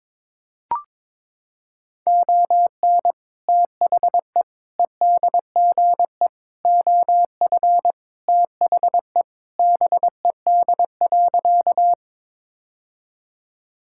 Morse code, heard as a keyed tone, about 22 wpm, 700 Hz. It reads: ON THE EDGE OF THE BED.
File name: practice / 22wpm / Sentences with 3-Word Phrases - 22wpm